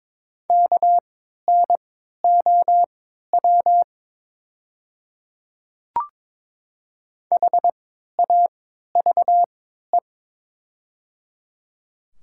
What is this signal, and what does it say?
Morse code, heard as a keyed tone, about 22 wpm, spaced as for 12 wpm, 690 Hz. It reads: KNOW HAVE
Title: 3x Character Spacing - Top 100 Words - 22wpm